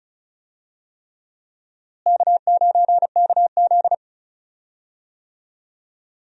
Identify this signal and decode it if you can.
Morse code, heard as a keyed tone, about 36 wpm, 700 Hz. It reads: K9KZ
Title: USA Call Signs - 4 Characters - Repeated 3x - 35wpm